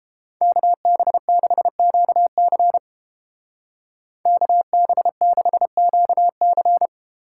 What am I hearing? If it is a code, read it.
Morse code, heard as a keyed tone, about 30 wpm, 700 Hz. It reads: KB6QC KB6QC